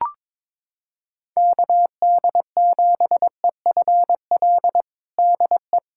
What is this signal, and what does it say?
Morse code, heard as a keyed tone, about 22 wpm, 700 Hz. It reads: KD7EFL DE
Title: USA Callsign DE Callsign - 6 Characters - 22wpm